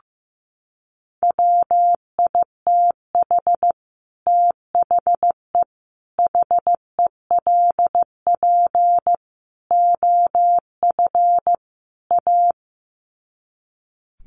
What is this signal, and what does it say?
Morse code, heard as a keyed tone, about 15 wpm, 700 Hz. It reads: WITH THE HELP OF A
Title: Sets of 5 Words 15wpm